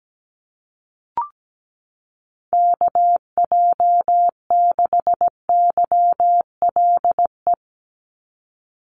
Morse code, heard as a keyed tone, about 17 wpm, 700 Hz. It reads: KJ6YLE
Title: USA Call Signs 17wpm